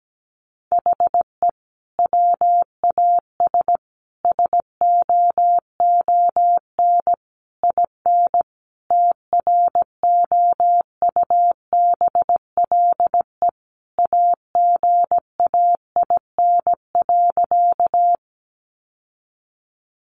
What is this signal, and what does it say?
Morse code, heard as a keyed tone, about 17 wpm, 700 Hz. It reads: HE WAS SOON IN TROUBLE AGAIN.
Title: Sentences from Top 1200 Words 17wpm